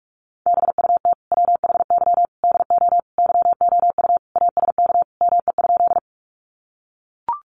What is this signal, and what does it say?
Morse code, heard as a keyed tone, about 44 wpm, 700 Hz. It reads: BUT WHY DO YOU ASK ME?
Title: Sentences from Top 200 Words 45wpm